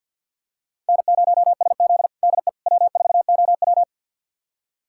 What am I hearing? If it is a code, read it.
Morse code, heard as a keyed tone, about 51 wpm, 690 Hz. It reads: N0SZ DE W4OW
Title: USA Callsign DE Callsign - 4 Characters - 50wpm